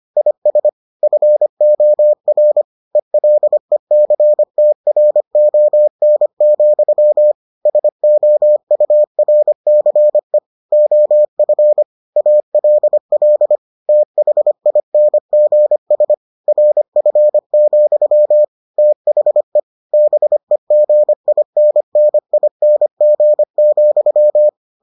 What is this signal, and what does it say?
Morse code, heard as a keyed tone, about 25 wpm, 600 Hz. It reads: IS FOR ELECTRON, SOURCE OF ALL THINGS RF, THE BEGINNING,